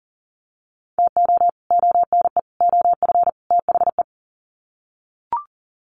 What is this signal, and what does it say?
Morse code, heard as a keyed tone, about 39 wpm, 700 Hz. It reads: TO ONE OF THE
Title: Sets of 4 Words 40wpm